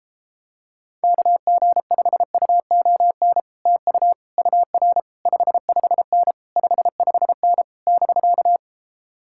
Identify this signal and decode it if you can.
Morse code, heard as a keyed tone, about 33 wpm, 700 Hz. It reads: KG5UON TU UR 55N 55N <BK>